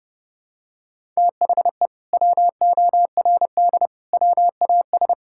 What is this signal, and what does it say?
Morse code, heard as a keyed tone, about 28 wpm, 700 Hz. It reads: THE WORD WAS
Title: Sentences from Top 1000 Words - Review Part 5 - 30wpm